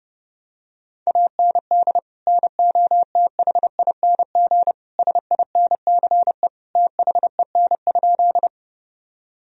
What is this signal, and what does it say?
Morse code, heard as a keyed tone, about 32 wpm, 700 Hz. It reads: AND NOTHING SINCE THEN?
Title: Sentences from Top 1000 Words - Review Part 1 - 30wpm